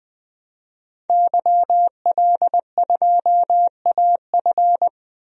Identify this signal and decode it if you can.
Morse code, heard as a keyed tone, about 21 wpm, 700 Hz. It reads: YL2AF
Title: DXCC-100 Call Signs - 5 Characters - 20wpm